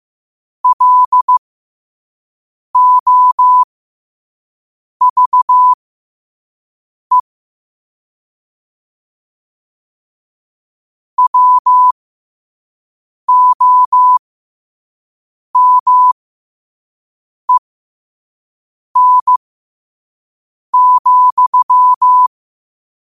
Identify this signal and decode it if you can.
Morse code, heard as a keyed tone, about 15 wpm, spaced as for 5 wpm, 1000 Hz. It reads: LOVE WOMEN,
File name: qotdfiles / 5wpm